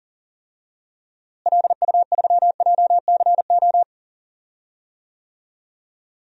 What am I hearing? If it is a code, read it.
Morse code, heard as a keyed tone, about 40 wpm, 700 Hz. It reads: LU3JCO
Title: DXCC-100 Call Signs - 6 Characters - Repeated 3x - 40wpm